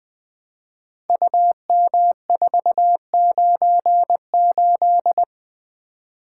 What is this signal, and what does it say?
Morse code, heard as a keyed tone, about 20 wpm, 700 Hz. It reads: UM498
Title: Code Groups - 5 Letter:Numbers - 20wpm